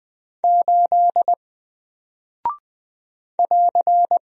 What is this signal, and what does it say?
Morse code, heard as a keyed tone, about 20 wpm, 700 Hz. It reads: 8 <AR>
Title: Single Character (Letter:Number:Punctuation:Prosign) 20wpm